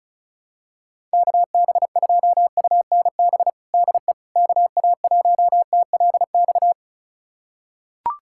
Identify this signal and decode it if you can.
Morse code, heard as a keyed tone, about 35 wpm, 700 Hz. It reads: KB2UNB DE KA1TLX
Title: USA Callsign DE Callsign - 6 Characters - 35wpm